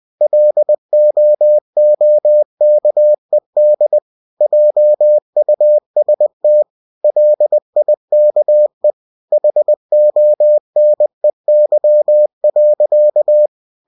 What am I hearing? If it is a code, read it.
Morse code, heard as a keyed tone, about 20 wpm, 600 Hz. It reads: LOOKED JUST LIKE HONEY.
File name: cw-books / winnie-the-pooh / winnie-20wpm-page-0037